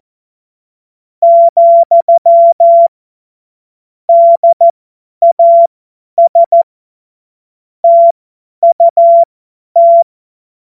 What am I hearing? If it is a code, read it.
Morse code, heard as a keyed tone, about 14 wpm, spaced as for 10 wpm, 680 Hz. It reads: , DAS TUT
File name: Morsekurs / Witze / Witz 17. Januar 2023 50 F70